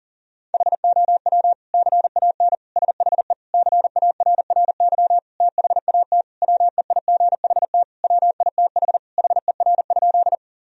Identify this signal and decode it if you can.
Morse code, heard as a keyed tone, about 40 wpm, 700 Hz. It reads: HOW CAN SHE CARRY THAT WEIGHT WITH HER?
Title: Sentences from Top 1000 Words - Review Part 3 - 40wpm